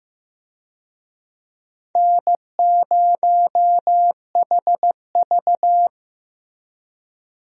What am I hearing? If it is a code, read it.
Morse code, heard as a keyed tone, about 15 wpm, 700 Hz. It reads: N0HV
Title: USA Call Signs - 4 Characters - Repeated 3x - 15wpm